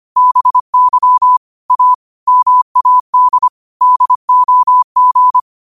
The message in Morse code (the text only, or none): BY A MAD DOG